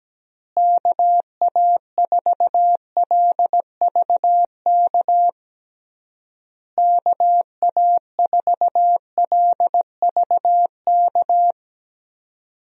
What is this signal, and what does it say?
Morse code, heard as a keyed tone, about 17 wpm, 700 Hz. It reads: KA4LVK KA4LVK
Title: USA Call Signs - 6 Characters - Repeated 3x - 17wpm